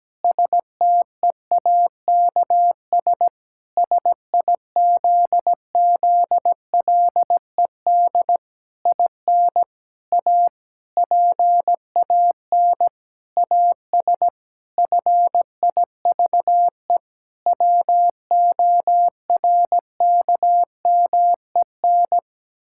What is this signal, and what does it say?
Morse code, heard as a keyed tone, about 17 wpm, 700 Hz. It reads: STEAKS SIZZLED IN A PAN AS FIVE WORKMEN
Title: Pangram Challenge v2 - 17wpm